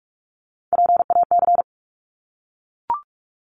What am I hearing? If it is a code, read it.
Morse code, heard as a keyed tone, about 44 wpm, 700 Hz. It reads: PAC
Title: ARRL Sections 45wpm